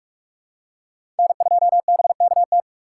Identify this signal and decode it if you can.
Morse code, heard as a keyed tone, about 46 wpm, 700 Hz. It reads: N2BKT
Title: USA Call Signs - 5 Characters - 4x Character Spacing - 45wpm